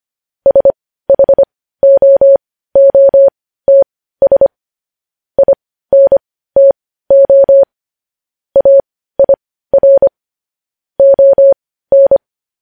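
Morse code, heard as a keyed tone, about 25 wpm, spaced as for 15 wpm, 550 Hz. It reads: SHOOTS INTO AIR ON
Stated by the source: Morse code news headlines broadcast.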